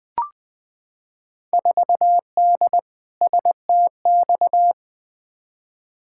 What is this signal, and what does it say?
Morse code, heard as a keyed tone, about 20 wpm, 700 Hz. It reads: 4D STX
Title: Field Day Exchanges Encore - 20wpm